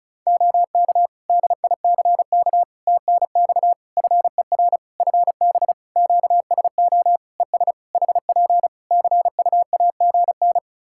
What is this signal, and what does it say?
Morse code, heard as a keyed tone, about 36 wpm, 700 Hz. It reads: OK DICK TNX FER FB QSO ES HP CUAGN
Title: Practice QSO - Exchange 4 - Ending - 35wpm